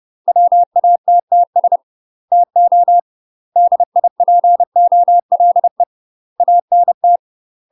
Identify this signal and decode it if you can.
Morse code, heard as a keyed tone, about 30 wpm, 700 Hz. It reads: WATTS TO DIPOLE ANT
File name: UK MASTER QSOs TRAINING / 30WPM / UK MASTER QSOs 0026